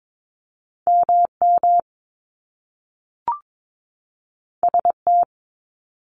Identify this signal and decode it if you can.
Morse code, heard as a keyed tone, about 22 wpm, 700 Hz. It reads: MM ST
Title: Top Two Letter Combinations in Words 22wpm